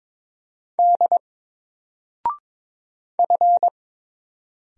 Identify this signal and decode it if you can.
Morse code, heard as a keyed tone, about 22 wpm, 700 Hz. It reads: D F
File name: Single-Letter-Number-22wpm